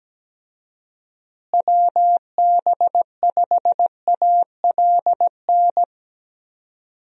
Morse code, heard as a keyed tone, about 17 wpm, 700 Hz. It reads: WB5ALN
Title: USA Call Signs - 6 Characters - Repeated 2x - 17wpm